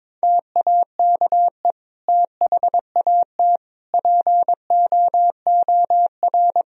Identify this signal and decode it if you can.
Morse code, heard as a keyed tone, about 21 wpm, 700 Hz. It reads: TAKE THAT POOR
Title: Sentences from Top 1000 Words - Review Part 2 - 22wpm